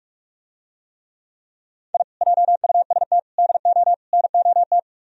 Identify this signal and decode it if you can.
Morse code, heard as a keyed tone, about 44 wpm, 700 Hz. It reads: I JUST DO NOT